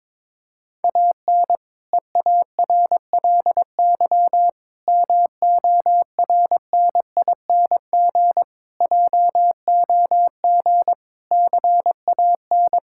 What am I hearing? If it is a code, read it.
Morse code, heard as a keyed tone, about 22 wpm, 700 Hz. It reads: AN EARLY MORNING JOG CAN
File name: Difficult Sentences with Hints - 22wpm